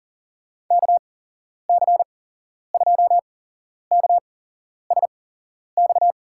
Morse code, heard as a keyed tone, about 37 wpm, spaced as for 11 wpm, 700 Hz. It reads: KC2KSX
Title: USA Call Signs - 8x Character Spacing - 40wpm